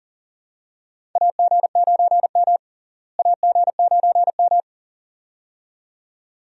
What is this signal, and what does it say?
Morse code, heard as a keyed tone, about 40 wpm, 700 Hz. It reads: AG9M AG9M
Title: USA Call Signs - 4 Characters - Repeated 2x - 40wpm